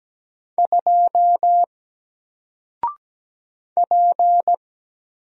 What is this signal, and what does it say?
Morse code, heard as a keyed tone, about 17 wpm, 700 Hz. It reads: 2 P
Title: Single Letter-Number Alt 17wpm